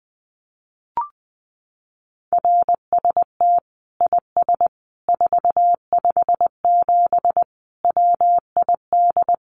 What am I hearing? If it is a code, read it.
Morse code, heard as a keyed tone, about 20 wpm, 700 Hz. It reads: RST IS 457 WID
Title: Common QSO Phrases 20wpm